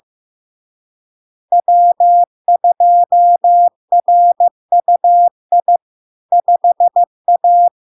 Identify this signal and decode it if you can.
Morse code, heard as a keyed tone, about 15 wpm, 700 Hz. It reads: W2RUI 5A
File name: LARK-FD-2022-Log-in-CW-15wpm-700-Hz